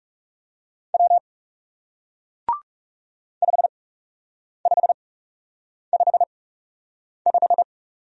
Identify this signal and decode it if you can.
Morse code, heard as a keyed tone, about 40 wpm, spaced as for 8 wpm, 700 Hz. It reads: W 5555